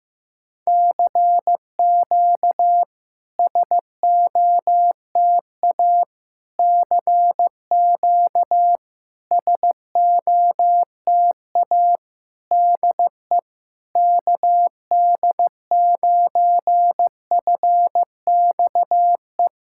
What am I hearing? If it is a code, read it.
Morse code, heard as a keyed tone, about 15 wpm, 700 Hz. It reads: CQ SOTA CQ SOTA DE KD9FXE
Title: SOTA QSO Practice 15wpm